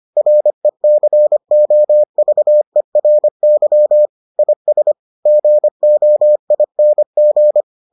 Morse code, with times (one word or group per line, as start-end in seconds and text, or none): 0.2-4.1 RECOVERY
4.4-4.9 IS
5.3-7.6 GOING